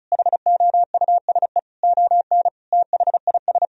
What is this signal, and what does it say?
Morse code, heard as a keyed tone, about 36 wpm, 700 Hz. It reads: HOUSE ON THIS
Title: Sentences from Top 1000 Words - Review Part 4 - 35wpm